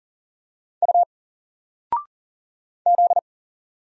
Morse code, heard as a keyed tone, about 40 wpm, 700 Hz. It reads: U Z
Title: Single Letter-Number 40wpm